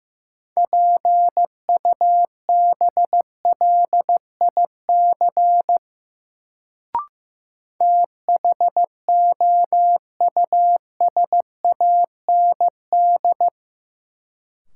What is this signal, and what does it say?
Morse code, heard as a keyed tone, about 16 wpm, 700 Hz. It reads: PUBLIC THOUSAND
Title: Sentences from Top 600 Words - New Word Drill - 15wpm